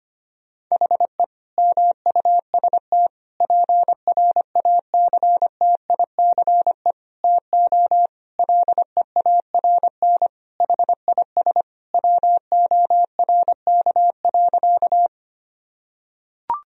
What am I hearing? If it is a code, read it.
Morse code, heard as a keyed tone, about 25 wpm, 700 Hz. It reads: HE MUST PRACTICE TO LEARN HIS WORK.
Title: Sentences from Top 1000 Words - Review Part 4 - 25wpm